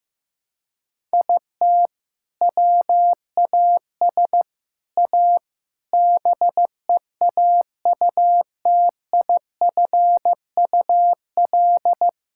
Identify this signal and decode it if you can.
Morse code, heard as a keyed tone, about 15 wpm, 700 Hz. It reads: IT WAS A BEAUTIFUL
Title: Sentences from Top 1000 Words - Review Part 5 - 15wpm